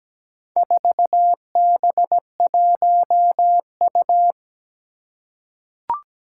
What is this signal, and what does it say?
Morse code, heard as a keyed tone, about 17 wpm, 700 Hz. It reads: 4B1U
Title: Four Challenging Letter-Number Combinations 17wpm